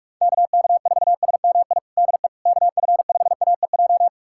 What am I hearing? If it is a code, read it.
Morse code, heard as a keyed tone, about 46 wpm, 700 Hz. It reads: KK4SMI DE KF5AEJ
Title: USA Callsign DE Callsign - 6 Characters - 45wpm